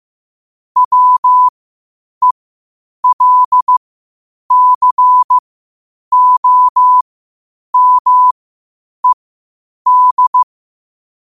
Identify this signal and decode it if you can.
Morse code, heard as a keyed tone, about 15 wpm, spaced as for 8 wpm, 1000 Hz. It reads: WELCOMED
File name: qotdfiles / 8wpm